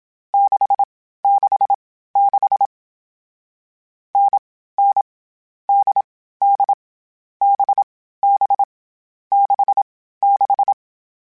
Morse code, heard as a keyed tone, about 27 wpm, 800 Hz. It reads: 6 6 6 N N D D B B 6 6
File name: Variable Speed / Variable Speed-mp3s / GZ GZ7 sp @25